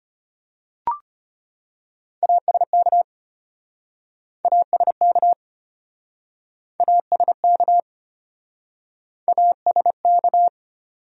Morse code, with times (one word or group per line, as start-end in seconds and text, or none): 2.2-3.0 ASK
4.5-5.3 ASK
6.8-7.8 ASK
9.3-10.5 ASK